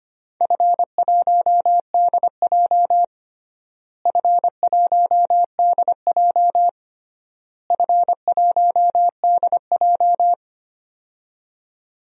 Morse code, heard as a keyed tone, about 25 wpm, 700 Hz. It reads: F1DJ F1DJ F1DJ
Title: DXCC-100 Call Signs - 4 Characters - Repeated 3x - 25wpm